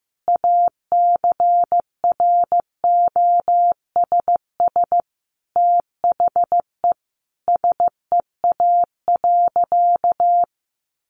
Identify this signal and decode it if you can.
Morse code, heard as a keyed tone, about 14 wpm, 700 Hz. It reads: ACROSS THE SEA.
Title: Sentences from Top 1300 Words 15wpm